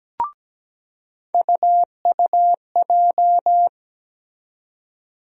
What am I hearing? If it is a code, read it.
Morse code, heard as a keyed tone, about 17 wpm, 700 Hz. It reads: UUJ
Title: USA Call Sign Postfixes - 3 Letters 17wpm